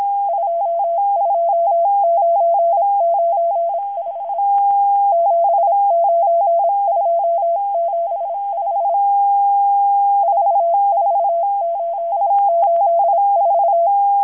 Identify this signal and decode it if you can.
Morse code, heard as a keyed tone, about 26 wpm, 680 Hz. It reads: 22995 79275 44884
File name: HM02- -0437utc-7351khz